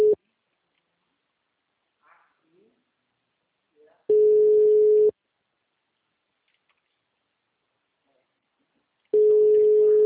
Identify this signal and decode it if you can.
Ringback tone (Europe, 425 Hz), with speech.
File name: out- -2031- - - .270